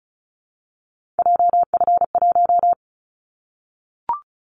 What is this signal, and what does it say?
Morse code, heard as a keyed tone, about 34 wpm, 700 Hz. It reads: JF1
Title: Callsign Prefixes for Top 25 DXCC - 35wpm